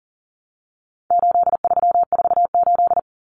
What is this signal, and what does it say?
Morse code, heard as a keyed tone, about 40 wpm, 700 Hz. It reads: 8348
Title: Common Cut Numbers - 4-Digit 40wpm